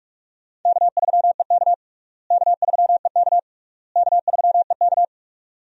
Morse code, heard as a keyed tone, about 46 wpm, 700 Hz. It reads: K3EK K3EK K3EK
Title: USA Call Signs - 4 Characters - Repeated 3x - 45wpm